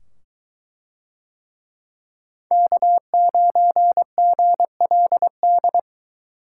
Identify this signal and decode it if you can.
Morse code, heard as a keyed tone, about 22 wpm, 700 Hz. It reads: K9GLD